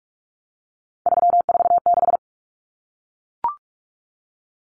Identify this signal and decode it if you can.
Morse code, heard as a keyed tone, about 44 wpm, 700 Hz. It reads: 346
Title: Common Cut Numbers - 3-Digit 45wpm